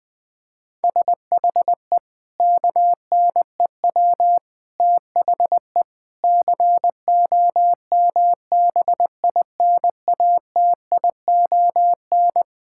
Morse code, heard as a keyed tone, about 21 wpm, 700 Hz. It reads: SHE KNEW THE COMBINATION
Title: Sentences from Top 1000 Words - Review Part 6 - 20wpm